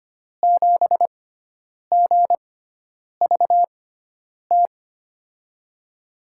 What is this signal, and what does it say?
Morse code, heard as a keyed tone, about 25 wpm, spaced as for 9 wpm, 700 Hz. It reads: 7GVT
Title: USA Call Signs - 6 Characters - 6x Character Spacing - 25wpm